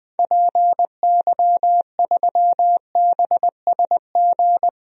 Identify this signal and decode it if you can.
Morse code, heard as a keyed tone, about 20 wpm, 700 Hz. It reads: PY3BSG
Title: DXCC-100 Call Signs - 6 Characters - 20wpm